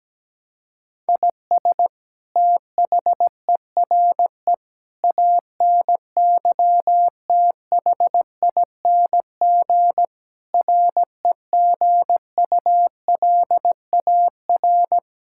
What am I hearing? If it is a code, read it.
Morse code, heard as a keyed tone, about 17 wpm, 700 Hz. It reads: IS THERE ANYTHING REGULAR
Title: Sentences from Top 1000 Words - Review Part 6 - 17wpm